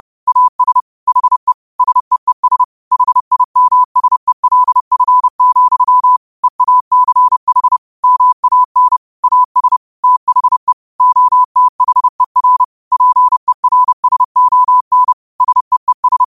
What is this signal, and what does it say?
Morse code, heard as a keyed tone, about 30 wpm, 1000 Hz. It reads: AS HE SEES HIMSELF, EACH MAN AS THE OTHER PERSON SEES
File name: qotdfiles / 30wpm